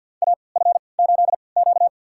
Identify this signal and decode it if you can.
Morse code, heard as a keyed tone, about 52 wpm, spaced as for 28 wpm, 700 Hz. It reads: AF8Q
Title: USA Call Signs - 3x Character Spacing - 50wpm